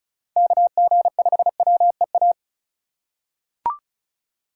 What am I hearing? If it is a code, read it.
Morse code, heard as a keyed tone, about 34 wpm, 700 Hz. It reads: KG5WEA
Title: USA Call Signs - 6 Characters - 2x Character Spacing - 35wpm